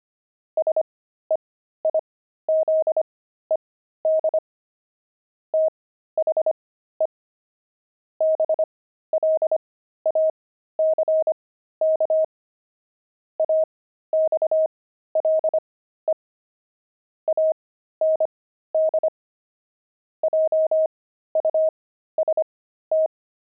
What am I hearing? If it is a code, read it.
Morse code, heard as a keyed tone, about 25 wpm, spaced as for 13 wpm, 630 Hz. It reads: SEIZED THE BLACK AXLE AND JUST
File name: pangram-fs-13-cs-25-axle